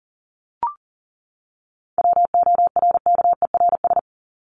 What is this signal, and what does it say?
Morse code, heard as a keyed tone, about 40 wpm, 700 Hz. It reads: WORKERS
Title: ES:S Suffix - Top 100 Words with Base Words 40wpm